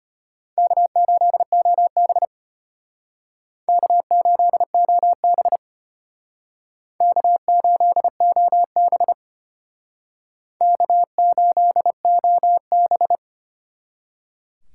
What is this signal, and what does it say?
Morse code, heard as a keyed tone, about 32 wpm, 700 Hz. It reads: K8OB K8OB K8OB K8OB